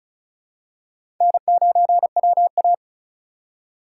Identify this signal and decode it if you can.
Morse code, heard as a keyed tone, about 36 wpm, 700 Hz. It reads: N9WA
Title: USA Call Signs - 4 Characters - 4x Character Spacing - 35wpm